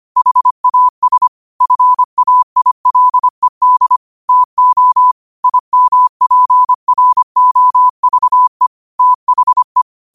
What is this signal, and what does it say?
Morse code, heard as a keyed tone, about 25 wpm, 1000 Hz. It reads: HAS FAILED TO IMPROVE THE
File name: qotdfiles / 25wpm